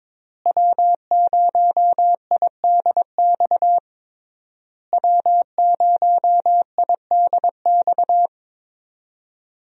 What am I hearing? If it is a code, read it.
Morse code, heard as a keyed tone, about 22 wpm, 700 Hz. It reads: W0IDX W0IDX
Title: USA Call Signs - 5 Characters - Repeated 3x - 22wpm